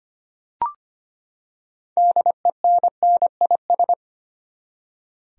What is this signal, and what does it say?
Morse code, heard as a keyed tone, about 25 wpm, 700 Hz. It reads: DENNIS
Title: USA Common Names - 25wpm